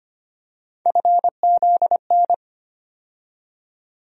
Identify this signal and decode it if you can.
Morse code, heard as a keyed tone, about 25 wpm, 700 Hz. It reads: FZN
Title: USA Call Sign Postfixes - 3 Letters 25wpm